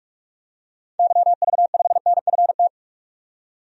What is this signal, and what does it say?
Morse code, heard as a keyed tone, about 47 wpm, 700 Hz. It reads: YV5NFT
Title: DXCC-100 Call Signs - 6 Characters - 45wpm